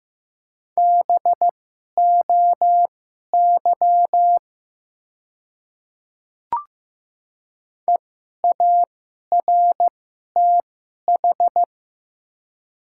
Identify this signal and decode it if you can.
Morse code, heard as a keyed tone, about 15 wpm, spaced as for 11 wpm, 700 Hz. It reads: BOY EARTH
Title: 2x Character Spacing - Top 200 Words - 15wpm